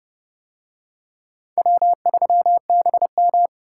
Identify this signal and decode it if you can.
Morse code, heard as a keyed tone, about 29 wpm, 700 Hz. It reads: W3BM